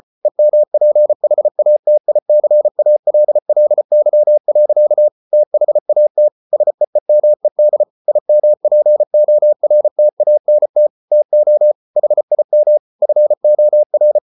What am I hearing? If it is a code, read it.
Morse code, heard as a keyed tone, about 34 wpm, 600 Hz. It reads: EMPHATICALLY. THAT SEEMED IMPORTANT TO HIM FOR